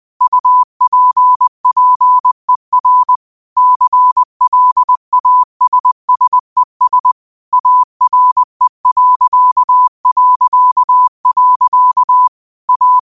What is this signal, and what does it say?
Morse code, heard as a keyed tone, about 20 wpm, 1000 Hz. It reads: UPPER CLASSES ARE... A
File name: qotdfiles / 20wpm